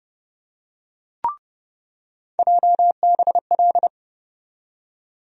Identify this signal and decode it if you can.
Morse code, heard as a keyed tone, about 30 wpm, 700 Hz. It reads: JBL
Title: USA Call Sign Postfixes - 3 Letters 30wpm